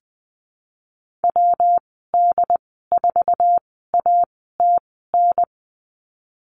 Morse code, heard as a keyed tone, about 20 wpm, spaced as for 15 wpm, 700 Hz. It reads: WD4ATN
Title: 147 Lesson 24g - Letter y - Call Signs 20wpm